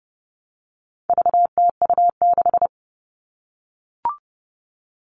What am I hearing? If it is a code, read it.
Morse code, heard as a keyed tone, about 30 wpm, 700 Hz. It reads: VTU6
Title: All Cut Numbers - 4-Digit 30wpm